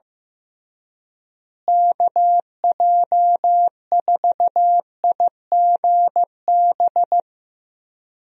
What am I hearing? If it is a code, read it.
Morse code, heard as a keyed tone, about 15 wpm, 700 Hz. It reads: KJ4IGB
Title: USA Call Signs - 6 Characters - Repeated 3x - 15wpm